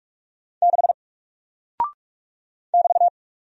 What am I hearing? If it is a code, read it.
Morse code, heard as a keyed tone, about 44 wpm, 700 Hz. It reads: 6 <BT>
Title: Single Character (Letter:Number:Punctuation:Prosign) Alt 45wpm